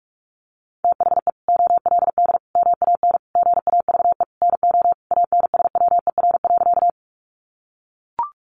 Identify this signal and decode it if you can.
Morse code, heard as a keyed tone, about 44 wpm, 700 Hz. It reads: THE OLD MAN GAVE NO ANSWER.
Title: Sentences from Top 500 Words 45wpm